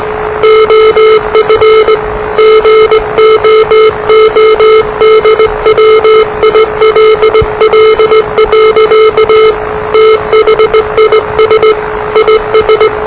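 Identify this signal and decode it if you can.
Morse code, heard as a keyed tone, about 20 wpm, 440 Hz, with noise. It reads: OF GOODWILL. THIS IS
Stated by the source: Morse challenge from NZ Net, the national amateur radio net of Aotearoa/New Zealand, which meets on 3535.0 kHz.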